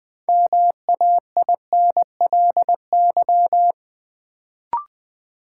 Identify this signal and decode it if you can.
Morse code, heard as a keyed tone, about 21 wpm, 700 Hz. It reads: MAINLY
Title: Sentences from Top 1400 Words - New Word Drill - 20wpm